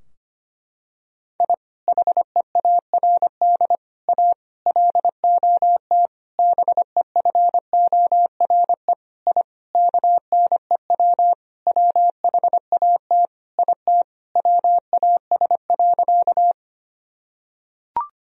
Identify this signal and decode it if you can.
Morse code, heard as a keyed tone, about 25 wpm, 700 Hz. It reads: I HEARD A LOT BEFORE I KNEW WHAT IT WAS.
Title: Sentences from Top 1000 Words - Review Part 3 - 25wpm